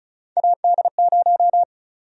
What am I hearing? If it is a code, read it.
Morse code, heard as a keyed tone, about 34 wpm, 700 Hz. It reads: AD0
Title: USA Call Sign Prefixes - 35wpm